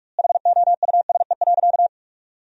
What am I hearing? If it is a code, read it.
Morse code, heard as a keyed tone, about 46 wpm, 700 Hz. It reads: HOUSE.
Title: 2x Word-Spacing - Sentences from Top 400 Words - 45wpm